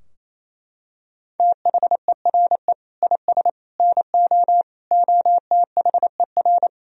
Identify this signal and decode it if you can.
Morse code, heard as a keyed tone, about 27 wpm, 700 Hz. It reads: THERE IS NO OTHER
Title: Sentences from Top 1000 Words - Review Part 2 - 28wpm